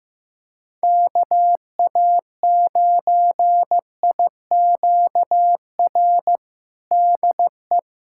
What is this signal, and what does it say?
Morse code, heard as a keyed tone, about 15 wpm, 700 Hz. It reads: KA9IQR DE
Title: USA Callsign DE Callsign - 6 Characters - 15wpm